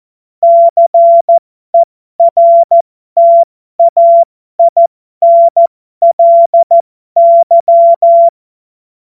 Morse code, heard as a keyed tone, about 14 wpm, spaced as for 12 wpm, 680 Hz. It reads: CERTAINLY
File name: Morsekurs / Witze